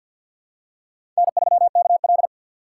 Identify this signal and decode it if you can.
Morse code, heard as a keyed tone, about 50 wpm, 700 Hz. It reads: N3KL
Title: USA Call Signs - 3x Character Spacing - 50wpm